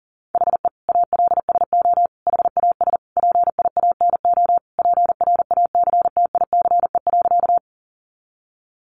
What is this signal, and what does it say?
Morse code, heard as a keyed tone, about 40 wpm, 700 Hz. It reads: HE ALSO HAS PIANO PRACTICE.
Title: Additional Easy Phrases 40wpm